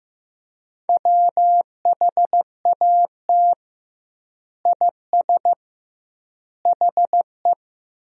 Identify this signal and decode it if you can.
Morse code, heard as a keyed tone, about 16 wpm, 700 Hz. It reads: WHAT IS HE
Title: 2x Word-Spacing - Sentences from Top 100 Words - 15wpm